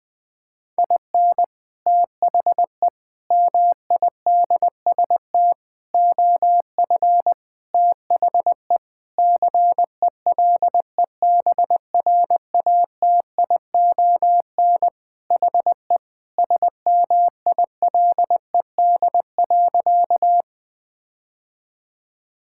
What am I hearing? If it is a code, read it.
Morse code, heard as a keyed tone, about 20 wpm, 700 Hz. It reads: IN THE MIDST OF THE CELEBRATION HE SMILED.
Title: Sentences with 5-Word Phrases - 20wpm